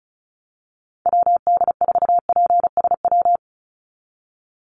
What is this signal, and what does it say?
Morse code, heard as a keyed tone, about 34 wpm, 700 Hz. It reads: WD4PSW
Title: USA Call Signs 35wpm